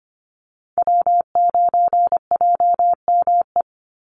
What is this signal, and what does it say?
Morse code, heard as a keyed tone, about 25 wpm, 700 Hz. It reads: W9JME
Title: USA Call Signs 25wpm